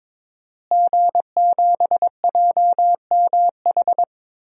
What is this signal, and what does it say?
Morse code, heard as a keyed tone, about 22 wpm, 700 Hz. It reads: G7JMH
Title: DXCC-100 Call Signs - 5 Characters - Repeated 3x - 22wpm